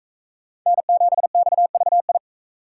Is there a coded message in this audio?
Yes: Morse code from 0.7 to 2.2 s, heard as a keyed tone, about 39 wpm, 700 Hz. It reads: N7XVI